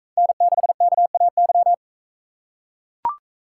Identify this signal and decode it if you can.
Morse code, heard as a keyed tone, about 44 wpm, 700 Hz. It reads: N6KAY